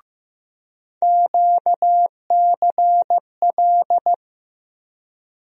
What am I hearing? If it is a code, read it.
Morse code, heard as a keyed tone, about 15 wpm, 700 Hz. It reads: QCL
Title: USA Call Sign Postfixes - 3 Letters 15wpm